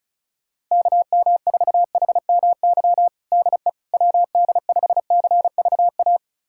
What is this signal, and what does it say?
Morse code, heard as a keyed tone, about 34 wpm, 700 Hz. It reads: KM4HMY DE WD5CVA
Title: USA Callsign DE Callsign - 6 Characters - 35wpm